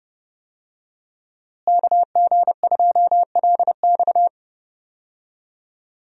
Morse code, heard as a keyed tone, about 30 wpm, 700 Hz. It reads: KG2LX
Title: USA Call Signs - 5 Characters - Repeated 3x - 30wpm